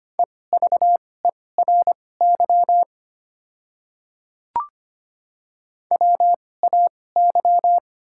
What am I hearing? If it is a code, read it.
Morse code, heard as a keyed tone, about 25 wpm, spaced as for 18 wpm, 700 Hz. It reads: EVERY WAY
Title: 2x Character Spacing - Top 200 Words - 25wpm